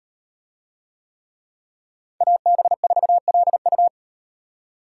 Morse code, heard as a keyed tone, about 38 wpm, 700 Hz. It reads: AB4LU